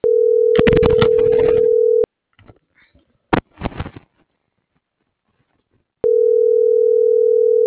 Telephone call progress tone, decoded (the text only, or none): ringback tone